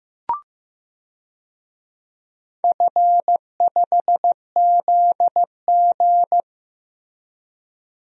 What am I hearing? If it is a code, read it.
Morse code, heard as a keyed tone, about 15 wpm, 700 Hz. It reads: F5ZG
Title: DXCC-100 Call Signs - 4 Characters - Repeated 2x - 15wpm